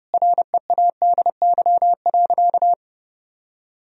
Morse code, heard as a keyed tone, about 30 wpm, 700 Hz. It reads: READY.
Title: Sentences from Top 1000 Words - Review Part 1 - 30wpm